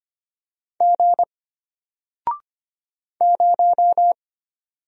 Morse code, heard as a keyed tone, about 25 wpm, 700 Hz. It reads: G 0
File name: Single-Letter-Number-25wpm